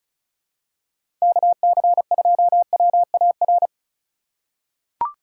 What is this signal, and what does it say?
Morse code, heard as a keyed tone, about 36 wpm, 700 Hz. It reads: KC2WAR
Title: USA Call Signs - 6 Characters - 3x Character Spacing - 35wpm